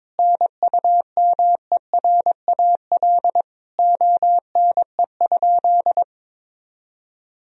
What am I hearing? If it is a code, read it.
Morse code, heard as a keyed tone, about 22 wpm, 700 Hz. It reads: NUMERAL ONE?
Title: Sentences from Top 1000 Words - Review Part 2 - 22wpm